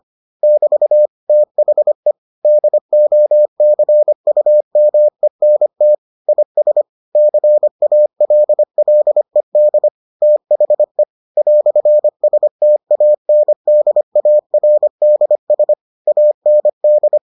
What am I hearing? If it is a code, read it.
Morse code, heard as a keyed tone, about 25 wpm, 600 Hz. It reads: <BT> THE DOCUMENT IS CALLED THE "STANDARDS AND